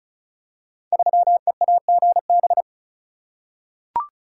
Morse code, heard as a keyed tone, about 36 wpm, 700 Hz. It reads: 3EAGB
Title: Code Groups - 5 Letter:Numbers - 35wpm